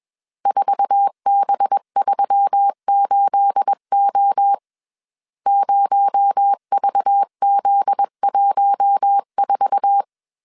Morse code, heard as a keyed tone, about 21 wpm, 800 Hz. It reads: 4B38O 0VZ14